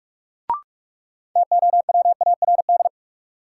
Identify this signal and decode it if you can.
Morse code, heard as a keyed tone, about 42 wpm, 700 Hz. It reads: TOWARD
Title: Sentences from Top 1300 Words - New Word Drill Alt - 45wpm